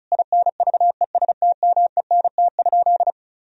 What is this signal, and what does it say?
Morse code, heard as a keyed tone, about 34 wpm, 700 Hz. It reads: INVESTMENT?
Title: Sentences from Top 1000 Words - Review Part 4 - 35wpm